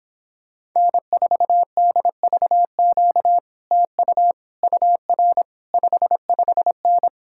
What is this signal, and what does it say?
Morse code, heard as a keyed tone, about 26 wpm, 700 Hz. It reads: N4DVQ TU UR 55N